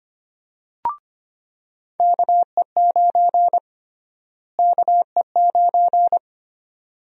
Morse code, heard as a keyed tone, about 25 wpm, 700 Hz. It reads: KE9 KE9
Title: USA Call Sign Prefixes - Repeated 2x - 25wpm